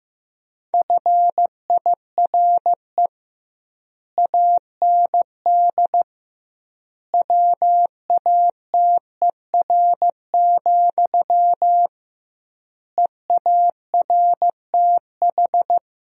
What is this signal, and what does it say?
Morse code, heard as a keyed tone, about 16 wpm, 700 Hz. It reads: FIRE AND WATER, EARTH
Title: 2x Word-Spacing - Sentences from Top 400 Words - 15wpm